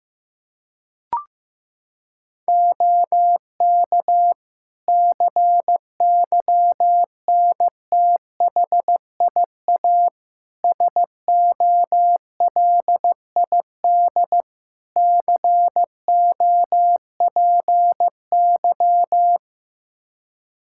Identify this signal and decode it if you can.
Morse code, heard as a keyed tone, about 15 wpm, 700 Hz. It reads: OK CYNTHIA SOLID COPY
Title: K1USN SST QSO Practice 15wpm